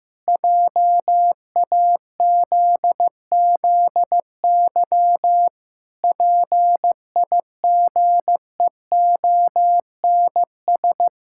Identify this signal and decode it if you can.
Morse code, heard as a keyed tone, about 15 wpm, 700 Hz. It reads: JAZZY PIGEONS
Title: Pangram Challenge v2 - 15wpm